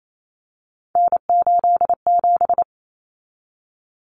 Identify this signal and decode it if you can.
Morse code, heard as a keyed tone, about 28 wpm, 700 Hz. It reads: N87
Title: Common Cut Numbers - 3-Digit 28wpm